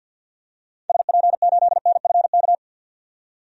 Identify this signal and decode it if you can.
Morse code, heard as a keyed tone, about 47 wpm, 700 Hz. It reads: SP8NFK
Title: DXCC-100 Call Signs - 6 Characters - 50wpm